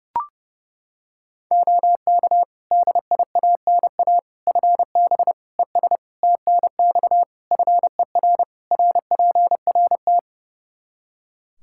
Morse code, heard as a keyed tone, about 30 wpm, 700 Hz. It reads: OK DIANA FB ES TNX FER RPRT
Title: Practice QSO - Exchange 2 - Info - 30wpm